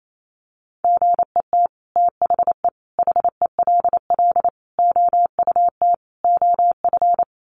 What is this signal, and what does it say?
Morse code, heard as a keyed tone, about 28 wpm, 700 Hz. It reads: GET THE HELL OUT OF
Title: Sets of 5 Words Encore2 28wpm